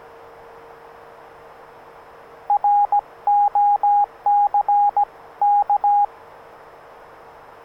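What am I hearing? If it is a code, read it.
Morse code, heard as a keyed tone, about 17 wpm, spaced as for 15 wpm, 820 Hz, with radio noise. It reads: ROCK
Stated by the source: Morse sent with a Rockmite by Dave Benson K1SWL, a QRP CW transceiver of about 1 W for 80, 40 or 20 m, usually on the QRP crystal frequency.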